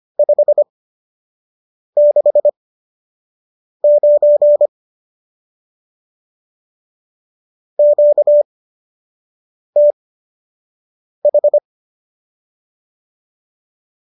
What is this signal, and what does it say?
Morse code, heard as a keyed tone, about 25 wpm, spaced as for 6 wpm, 590 Hz. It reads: 569 QTH